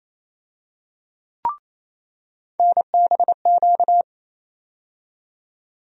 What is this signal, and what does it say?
Morse code, heard as a keyed tone, about 28 wpm, 700 Hz. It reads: NBQ
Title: USA Call Sign Postfixes - 3 Letters 28wpm